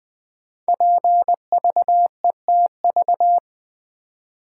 Morse code, heard as a keyed tone, about 20 wpm, 700 Hz. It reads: PVETV
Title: Code Groups - 5 Letters - 20wpm